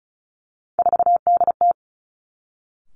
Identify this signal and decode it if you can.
Morse code, heard as a keyed tone, about 34 wpm, 700 Hz. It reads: 4DT
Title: Three Challenging Letter-Number Combinations 35wpm